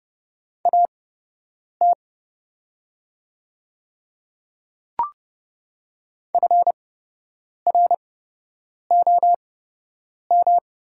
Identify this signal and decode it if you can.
Morse code, heard as a keyed tone, about 31 wpm, spaced as for 8 wpm, 690 Hz. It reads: AT FROM
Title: 8x Character Spacing - Top 200 Words - 30wpm